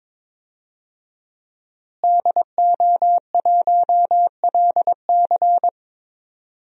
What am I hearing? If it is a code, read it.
Morse code, heard as a keyed tone, about 22 wpm, 700 Hz. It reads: DO1LC
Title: DXCC-100 Call Signs - 5 Characters - Repeated 2x - 22wpm